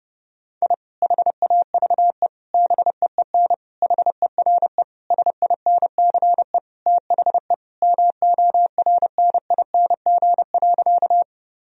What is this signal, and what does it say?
Morse code, heard as a keyed tone, about 30 wpm, 700 Hz. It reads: I HAVE BEEN HERE SINCE THE MORNING.
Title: Sentences from Top 1000 Words - Review Part 1 - 30wpm